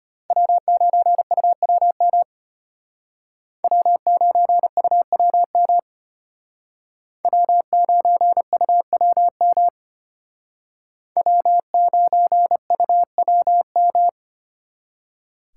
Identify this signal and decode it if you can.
Morse code, heard as a keyed tone, about 32 wpm, 700 Hz. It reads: W9UWM W9UWM W9UWM W9UWM